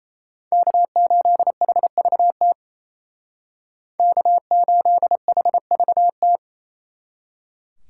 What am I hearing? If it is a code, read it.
Morse code, heard as a keyed tone, about 33 wpm, 700 Hz. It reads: K8HVT K8HVT
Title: Speed Racing - USA 1x3 Call Signs 28wpm